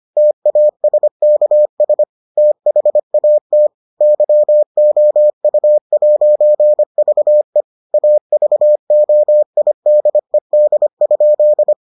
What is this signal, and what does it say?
Morse code, heard as a keyed tone, about 25 wpm, 600 Hz. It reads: TASKS THAT YOU'VE AVOIDED?